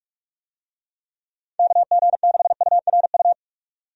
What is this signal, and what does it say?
Morse code, heard as a keyed tone, about 43 wpm, 700 Hz. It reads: KG6URU
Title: USA Call Signs - 6 Characters - 45wpm